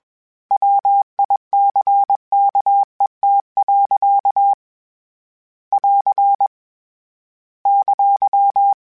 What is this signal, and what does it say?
Morse code, heard as a keyed tone, about 21 wpm, 800 Hz. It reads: WICKET. <AR> !
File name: Misc / Misc-mp3s / Word Pairs @20